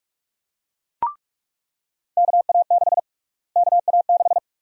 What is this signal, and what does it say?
Morse code, heard as a keyed tone, about 44 wpm, 700 Hz. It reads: KA6 KA6
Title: USA Call Sign Prefixes - Repeated 2x - 45wpm